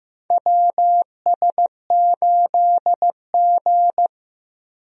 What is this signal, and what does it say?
Morse code, heard as a keyed tone, about 14 wpm, 700 Hz. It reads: WS8G
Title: USA Call Signs - 4 Characters - Repeated 3x - 15wpm